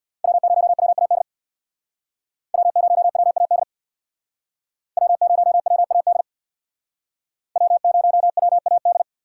Morse code, heard as a keyed tone, about 62 wpm, 700 Hz. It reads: W0WAD W0WAD W0WAD W0WAD
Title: Speed Racing - USA Call Signs 50wpm